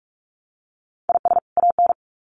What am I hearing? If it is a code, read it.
Morse code, heard as a keyed tone, about 46 wpm, 700 Hz. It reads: IS AN
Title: Sets of 2 Words 45wpm